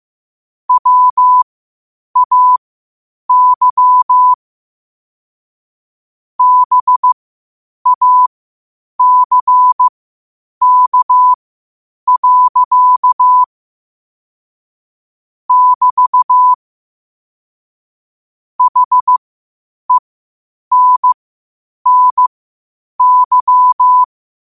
Morse code, heard as a keyed tone, about 15 wpm, spaced as for 8 wpm, 1000 Hz. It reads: WAY BACK. <BT> HENNY
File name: qotdfiles / 8wpm